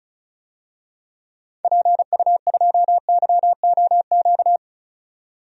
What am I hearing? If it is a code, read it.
Morse code, heard as a keyed tone, about 36 wpm, 700 Hz. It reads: PU2YOQ
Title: DXCC-100 Call Signs - 6 Characters - 35wpm